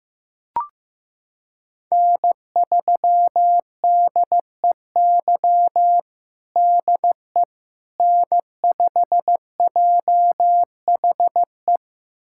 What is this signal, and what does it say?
Morse code, heard as a keyed tone, about 16 wpm, 700 Hz. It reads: N3DEY DE N5JHE
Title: USA Callsign DE Callsign - 5 Characters - 15wpm